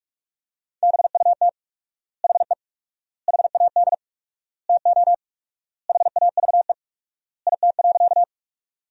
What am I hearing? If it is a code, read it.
Morse code, heard as a keyed tone, about 46 wpm, 700 Hz. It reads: BUT HE HAD TO HAVE IT.
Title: 4x Word-Spacing - Sentences from Top 100 Words - 45wpm